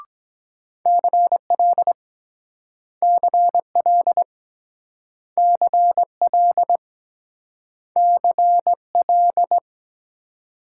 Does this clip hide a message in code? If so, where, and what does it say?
Yes: Morse code from 0.9 to 9.6 s, heard as a keyed tone, about 23 wpm, 700 Hz. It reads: CL CL CL CL